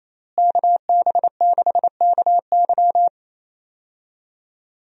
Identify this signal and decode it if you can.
Morse code, heard as a keyed tone, about 28 wpm, 700 Hz. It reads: KB6KY
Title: USA Call Signs - 5 Characters - Repeated 3x - 28wpm